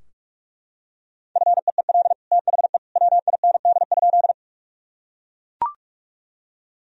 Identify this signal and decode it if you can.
Morse code, heard as a keyed tone, about 41 wpm, 700 Hz. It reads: FEEL THE WIND?
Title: Sentences from Top 1000 Words - Review Part 2 - 45wpm